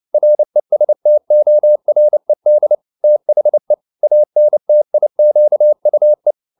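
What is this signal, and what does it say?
Morse code, heard as a keyed tone, about 29 wpm, 600 Hz. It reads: RESTORED THE ANTIQUE
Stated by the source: Morse code at 29 wpm, sending Ken Follett's novel Never in English, from CW med Gnister.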